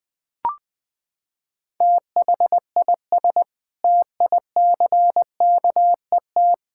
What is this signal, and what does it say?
Morse code, heard as a keyed tone, about 20 wpm, 700 Hz. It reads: THIS TICKET
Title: Sentences with 7-Word Phrases - 20wpm